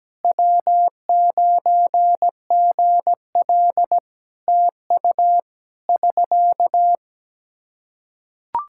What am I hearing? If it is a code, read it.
Morse code, heard as a keyed tone, about 17 wpm, 700 Hz. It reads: W9GL TU <SK>
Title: Practice QSO - Exchange 4 - Ending - 17wpm